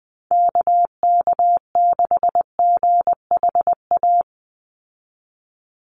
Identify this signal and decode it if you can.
Morse code, heard as a keyed tone, about 19 wpm, 700 Hz. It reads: KK6GHA
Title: Call Signs - 20wpm